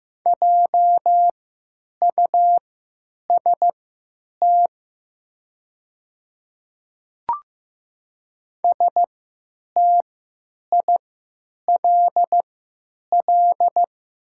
Morse code, heard as a keyed tone, about 15 wpm, spaced as for 9 wpm, 700 Hz. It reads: JUST STILL
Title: 3x Character Spacing - Top 300 Words - 15wpm